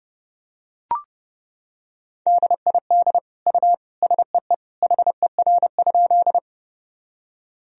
Morse code, heard as a keyed tone, about 30 wpm, 700 Hz. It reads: DID U SEE HER?
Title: Sentences from Top 100 Words and Abbreviations 30wpm